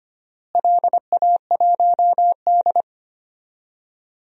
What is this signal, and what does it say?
Morse code, heard as a keyed tone, about 25 wpm, 700 Hz. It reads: LA1D
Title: DXCC-100 Call Signs - 4 Characters - Repeated 3x - 25wpm